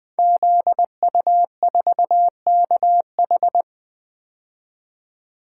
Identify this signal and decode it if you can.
Morse code, heard as a keyed tone, about 20 wpm, 700 Hz. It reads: ZU4KH
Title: Code Groups - 5 Letter:Numbers - 20wpm